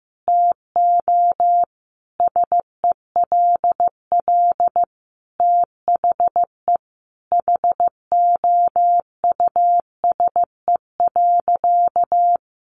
Morse code, heard as a keyed tone, about 15 wpm, 700 Hz. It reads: TO SELL THE HOUSE.
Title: Sentences from Top 1500 Words 15wpm